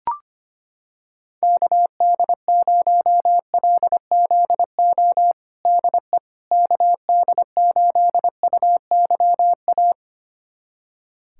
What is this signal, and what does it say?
Morse code, heard as a keyed tone, about 25 wpm, 700 Hz. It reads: KD0LZO DE KD8UYA
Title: USA Callsign DE Callsign - 6 Characters - 25wpm